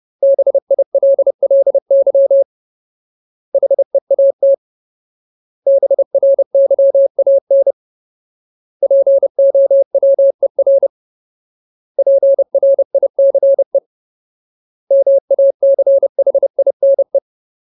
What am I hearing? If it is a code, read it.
Morse code, heard as a keyed tone, about 30 wpm, 550 Hz. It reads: BILLY HEAT BRYAN POWER PRICE MACHINE
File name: Ultimate Word List / 30WPM Ultimate